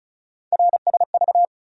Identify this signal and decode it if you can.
Morse code, heard as a keyed tone, about 36 wpm, 700 Hz. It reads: RSV
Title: USA Call Sign Postfixes - 3 Letters 35wpm